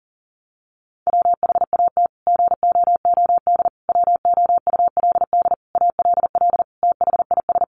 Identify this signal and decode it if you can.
Morse code, heard as a keyed tone, about 40 wpm, 700 Hz. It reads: WHAT GOOD WOULD ALL THIS
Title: Sentences from Top 1200 Words 40wpm